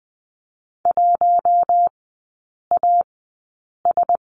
Morse code, heard as a keyed tone, about 20 wpm, spaced as for 8 wpm, 700 Hz. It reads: 1AS
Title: 047 Lesson 9e - Number 4 - Three Characters 20wpm